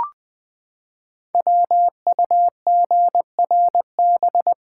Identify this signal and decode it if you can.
Morse code, heard as a keyed tone, about 20 wpm, 700 Hz. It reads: WUGRB
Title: Code Groups - 5 Letters - 20wpm